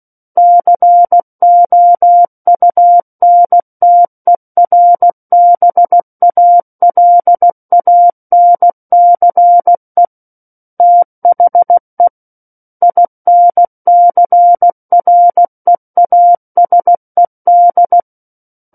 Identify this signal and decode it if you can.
Morse code, heard as a keyed tone, about 16 wpm, 700 Hz. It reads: COUNTERBALANCE THE INCREASED
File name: Book One / 07-Chapter 7 - 16 WPM